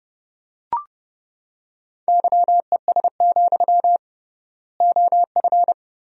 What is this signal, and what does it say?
Morse code, heard as a keyed tone, about 32 wpm, 700 Hz. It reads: YES, OF
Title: 3x Word-Spacing - Sentences from Top 500 Words - 30wpm